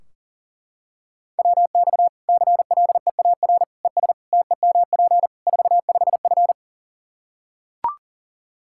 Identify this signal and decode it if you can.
Morse code, heard as a keyed tone, about 40 wpm, 700 Hz. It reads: WX CLEAR ES TEMP 45F
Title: Practice QSO - Exchange 2 - Info - 40wpm